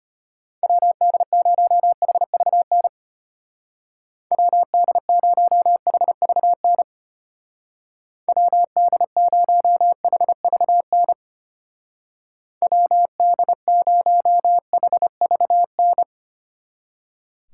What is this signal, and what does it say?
Morse code, heard as a keyed tone, about 31 wpm, 700 Hz. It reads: WD0HVN WD0HVN WD0HVN WD0HVN
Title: Speed Racing - USA Call Signs 25wpm